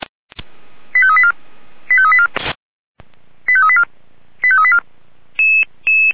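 Five-tone selective call (ZVEI), heard as quick runs of five tones, tones of about 70 ms. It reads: 85384 85384 85384 85384
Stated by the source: ZVEI five-tone selective call 89155 received on SDR.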